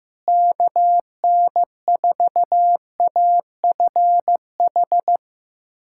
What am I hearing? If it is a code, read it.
Morse code, heard as a keyed tone, about 16 wpm, 700 Hz. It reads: KN4AFH
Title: USA Callsign DE Callsign 15wpm